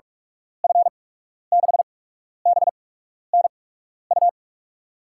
Morse code, heard as a keyed tone, about 46 wpm, spaced as for 12 wpm, 700 Hz. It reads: F6BNU
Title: USA Call Signs - 6 Characters - 8x Character Spacing - 45wpm